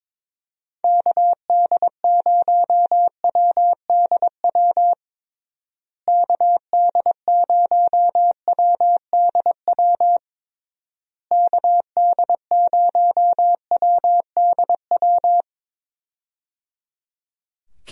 Morse code, heard as a keyed tone, about 22 wpm, 700 Hz. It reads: KD0WDW KD0WDW KD0WDW
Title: USA Call Signs - 6 Characters - Repeated 3x - 22wpm